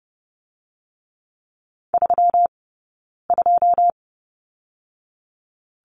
Morse code, heard as a keyed tone, about 30 wpm, spaced as for 9 wpm, 700 Hz. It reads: 32
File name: Lesson 20d - Number 3 - Two Characters 30wpm